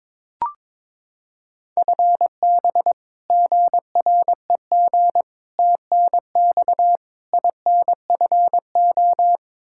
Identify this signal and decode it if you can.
Morse code, heard as a keyed tone, about 21 wpm, 700 Hz. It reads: FB GREG TNX INFO
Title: Practice QSO - Exchange 3 - More Info - 22wpm